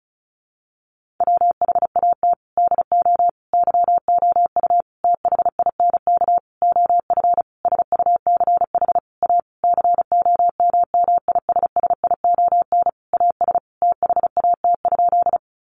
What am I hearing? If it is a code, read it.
Morse code, heard as a keyed tone, about 34 wpm, 700 Hz. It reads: WHAT DO YOU THINK OF SUCH A COMMISSION AS THAT?
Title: Sentences from Top 1400 Words 35wpm